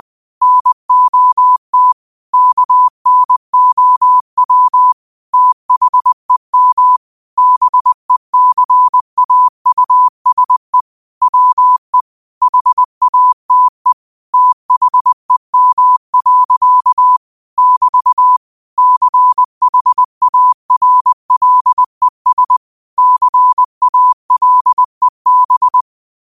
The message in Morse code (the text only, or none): NOT KNOW THEM BECAUSE WE HATE THEM. <BT> CHARLES CALEB